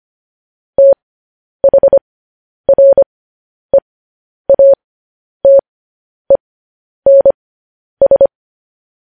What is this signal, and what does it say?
Morse code, heard as a keyed tone, about 25 wpm, spaced as for 10 wpm, 550 Hz. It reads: THREATENS
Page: news headlines in morse code at 10 wpm